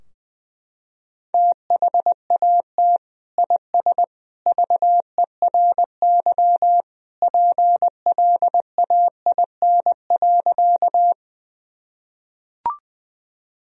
Morse code, heard as a keyed tone, about 21 wpm, 700 Hz. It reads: THAT IS VERY PLAIN.
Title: Sentences from Top 1000 Words - Review Part 2 - 20wpm